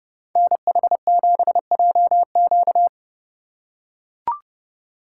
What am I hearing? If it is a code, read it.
Morse code, heard as a keyed tone, about 30 wpm, 700 Hz. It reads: NH7JQ
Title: USA Call Signs - 8x Character Spacing - 30wpm